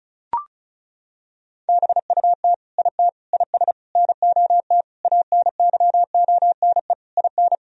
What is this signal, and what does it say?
Morse code, heard as a keyed tone, about 36 wpm, 700 Hz. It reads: BUT IT IS NOT ANYONE IN
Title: Sentences from Top 1000 Words - Review Part 4 - 35wpm